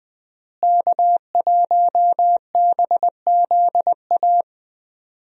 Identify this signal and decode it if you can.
Morse code, heard as a keyed tone, about 20 wpm, 700 Hz. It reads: K1BZA
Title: USA Call Signs - 5 Characters - Repeated 3x - 20wpm